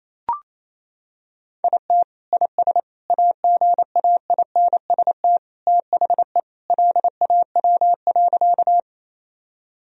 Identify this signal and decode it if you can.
Morse code, heard as a keyed tone, about 28 wpm, 700 Hz. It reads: IT IS AGAINST THE LAW.
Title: Sentences from Top 1000 Words - Review Part 3 - 28wpm